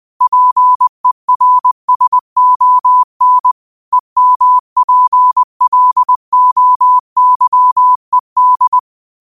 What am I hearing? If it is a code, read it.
Morse code, heard as a keyed tone, about 20 wpm, 1000 Hz. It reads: PERSON EMPLOYED
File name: qotdfiles / 20wpm